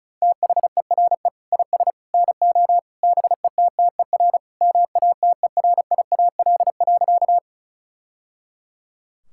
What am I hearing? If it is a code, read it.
Morse code, heard as a keyed tone, about 36 wpm, 700 Hz. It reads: THERE IS NO BETTER MATERIAL.
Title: Sentences from Top 1000 Words - Review Part 5 - 35wpm